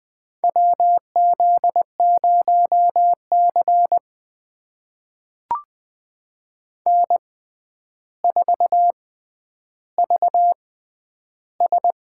Morse code, heard as a keyed tone, about 20 wpm, spaced as for 10 wpm, 700 Hz. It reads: WZ0C N4VS
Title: USA Call Signs - 4 Characters - 6x Character Spacing - 20wpm